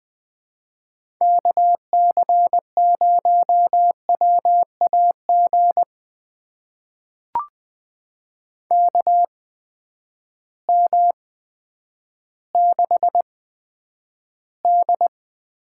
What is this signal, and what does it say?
Morse code, heard as a keyed tone, about 20 wpm, spaced as for 10 wpm, 700 Hz. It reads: KC0WAG KM6D
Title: USA Call Signs - 6 Characters - 8x Character Spacing - 20wpm